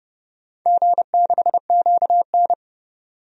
Morse code, heard as a keyed tone, about 30 wpm, 700 Hz. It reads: G6QN
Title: DXCC-100 Call Signs - 4 Characters - 30wpm